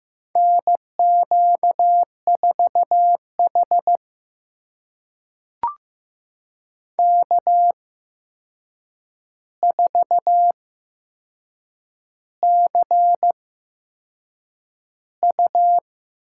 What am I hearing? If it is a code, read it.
Morse code, heard as a keyed tone, about 15 wpm, spaced as for 6 wpm, 700 Hz. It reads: NQ4H K4CU
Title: USA Call Signs - 4 Characters - 8x Character Spacing - 15wpm